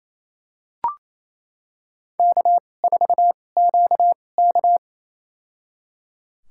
Morse code, heard as a keyed tone, about 28 wpm, spaced as for 20 wpm, 700 Hz. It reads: K4QK
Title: USA Call Signs - 4 Characters - 2x Character Spacing - 28wpm